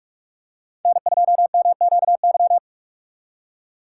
Morse code, heard as a keyed tone, about 43 wpm, 700 Hz. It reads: N2MQY
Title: USA Call Signs - 5 Characters - 3x Character Spacing - 45wpm